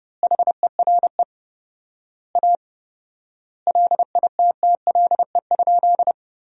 Morse code, heard as a keyed tone, about 30 wpm, 700 Hz. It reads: HERE A LITTLE?
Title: 4x Word-Spacing - Sentences from Top 500 Words - 30wpm